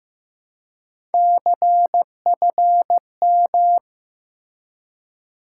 Morse code, heard as a keyed tone, about 15 wpm, 700 Hz. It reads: CFM
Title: Top 90 QSO Elements 15wpm